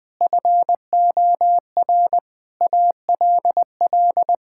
Morse code, heard as a keyed tone, about 20 wpm, 700 Hz. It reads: FOR ALL
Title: Sentences with 5-Word Phrases - 20wpm